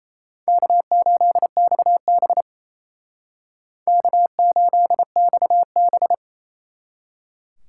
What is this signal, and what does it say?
Morse code, heard as a keyed tone, about 36 wpm, 700 Hz. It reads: K8XB K8XB